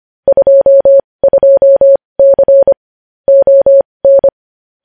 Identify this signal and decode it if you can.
Morse code, heard as a keyed tone, about 25 wpm, spaced as for 20 wpm, 550 Hz. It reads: 22C ON
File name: podcast / News headlines-1x-25x20